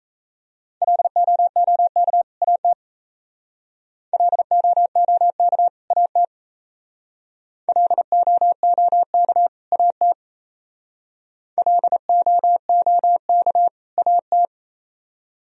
Morse code, heard as a keyed tone, about 35 wpm, 700 Hz. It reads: LOOK AT LOOK AT LOOK AT LOOK AT